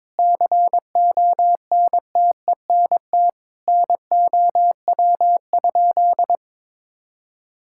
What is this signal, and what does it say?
Morse code, heard as a keyed tone, about 22 wpm, 700 Hz. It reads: CONTENT NOW?
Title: Sentences from Top 1000 Words - Review Part 6 - 22wpm